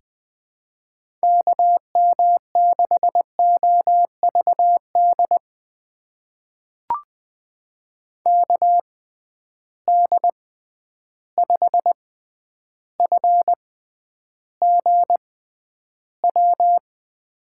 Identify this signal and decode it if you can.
Morse code, heard as a keyed tone, about 20 wpm, spaced as for 10 wpm, 700 Hz. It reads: KM6OVD KD5FGW
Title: USA Call Signs - 6 Characters - 6x Character Spacing - 20wpm